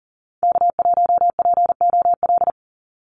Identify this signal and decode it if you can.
Morse code, heard as a keyed tone, about 38 wpm, 700 Hz. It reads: K1POL
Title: USA Call Signs 40wpm